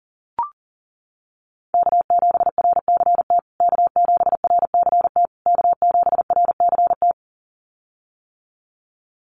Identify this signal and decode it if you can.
Morse code, heard as a keyed tone, about 40 wpm, 700 Hz. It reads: K7RCT K7RCT K7RCT
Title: USA Call Signs Repeated 3x 40wpm